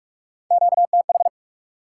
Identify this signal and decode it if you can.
Morse code, heard as a keyed tone, about 46 wpm, 700 Hz. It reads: QTH
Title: Top 90 QSO Elements 45wpm